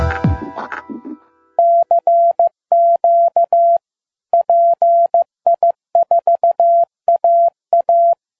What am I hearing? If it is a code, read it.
Morse code, heard as a keyed tone, about 15 wpm, 680 Hz. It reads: CQ PI4AA
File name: pi4aa / PI4AA Uitzending